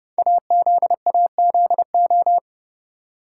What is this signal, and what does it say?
Morse code, heard as a keyed tone, about 30 wpm, 700 Hz. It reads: AZAZO
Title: Code Groups - 5 Letters - 30wpm